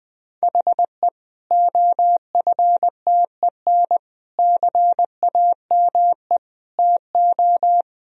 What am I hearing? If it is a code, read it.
Morse code, heard as a keyed tone, about 20 wpm, 700 Hz. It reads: HE OFTEN CAME TO
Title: Sentences from Top 1000 Words - Review Part 1 - 20wpm